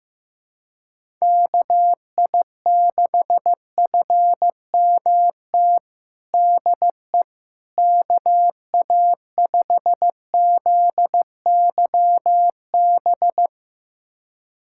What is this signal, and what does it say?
Morse code, heard as a keyed tone, about 15 wpm, 700 Hz. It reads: KI6FMT DE KA5ZYB
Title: USA Callsign DE Callsign - 6 Characters - 15wpm